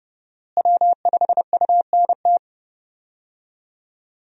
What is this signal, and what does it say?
Morse code, heard as a keyed tone, about 32 wpm, 700 Hz. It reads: W5UNT
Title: USA Call Signs - 5 Characters - Repeated 3x - 30wpm